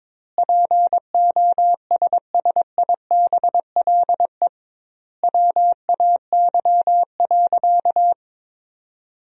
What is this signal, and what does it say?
Morse code, heard as a keyed tone, about 22 wpm, 700 Hz. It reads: POSSIBLE WAY.
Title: 2x Word-Spacing - Sentences from Top 500 Words - 22wpm